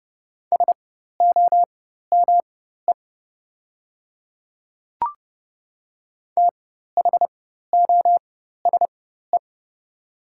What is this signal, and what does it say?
Morse code, heard as a keyed tone, about 31 wpm, spaced as for 14 wpm, 700 Hz. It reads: SOME THOSE
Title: 4x Character Spacing - Top 300 Words - 30wpm